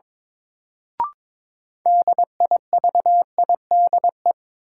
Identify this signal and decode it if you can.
Morse code, heard as a keyed tone, about 22 wpm, 700 Hz. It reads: DIVIDE
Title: Sentences from Top 1600 Words - New Word Drill - 22wpm